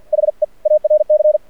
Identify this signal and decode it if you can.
Morse code, heard as a keyed tone, about 48 wpm, 610 Hz. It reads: HE ARC
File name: archived / CRSnet